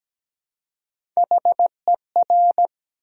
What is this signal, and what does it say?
Morse code, heard as a keyed tone, about 17 wpm, 700 Hz. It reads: HER
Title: 3x Word-Spacing - Sentences from Top 400 Words - 17wpm